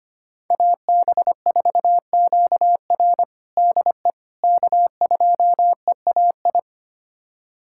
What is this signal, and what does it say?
Morse code, heard as a keyed tone, about 25 wpm, 700 Hz. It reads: AB4QR DE K2EAI
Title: USA Callsign DE Callsign - 5 Characters - 25wpm